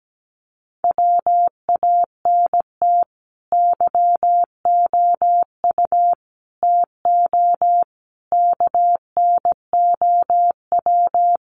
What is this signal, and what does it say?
Morse code, heard as a keyed tone, about 17 wpm, 700 Hz. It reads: WANT YOU TO KNOW
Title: Sets of 5 Words Encore 17wpm